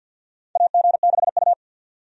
Morse code, heard as a keyed tone, about 51 wpm, 700 Hz. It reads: AG6U
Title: USA Call Signs - 4 Characters - 50wpm